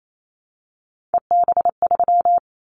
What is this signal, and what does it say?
Morse code, heard as a keyed tone, about 28 wpm, 700 Hz. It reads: EB3
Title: Callsign Prefixes for Top 25 DXCC - 28wpm